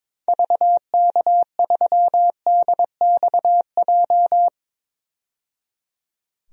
Morse code, heard as a keyed tone, about 22 wpm, 700 Hz. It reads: VK3DXJ
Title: DXCC-100 Call Signs - 6 Characters - 22wpm